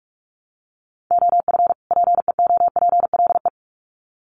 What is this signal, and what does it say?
Morse code, heard as a keyed tone, about 44 wpm, 700 Hz. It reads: OF PEOPLE
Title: Sets of 2 Words 45wpm